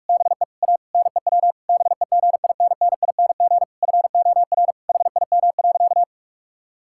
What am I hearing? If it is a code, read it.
Morse code, heard as a keyed tone, about 41 wpm, 700 Hz. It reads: BE A NEW BEGINNING FOR HIM.
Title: Sentences from Top 1000 Words - Review Part 5 - 45wpm